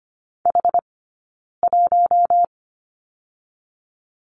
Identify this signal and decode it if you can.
Morse code, heard as a keyed tone, about 25 wpm, 700 Hz. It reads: H 1
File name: course / 25:10wpm / 099 Lesson 17d - Letter u - Two Characters 25wpm